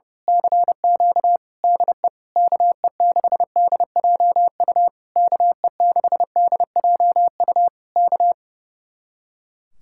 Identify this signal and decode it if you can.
Morse code, heard as a keyed tone, about 29 wpm, 700 Hz. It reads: CQ DE KE6DJU KE6DJU K
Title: Speed-Racing Common QSO Phrases Encore - 30wpm